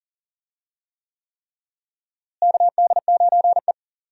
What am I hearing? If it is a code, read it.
Morse code, heard as a keyed tone, about 40 wpm, 700 Hz. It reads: KD9E
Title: USA Call Signs - 4 Characters - Repeated 3x - 40wpm